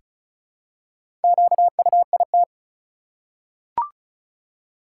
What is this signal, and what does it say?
Morse code, heard as a keyed tone, about 36 wpm, 700 Hz. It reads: QUIT
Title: Four Letter Words - 35wpm